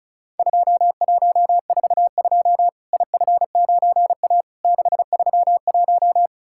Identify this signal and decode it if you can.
Morse code, heard as a keyed tone, about 36 wpm, 700 Hz. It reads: 2142 IF9A 631
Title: Worked All Europe DX Contest - QTC Exchange - 35wpm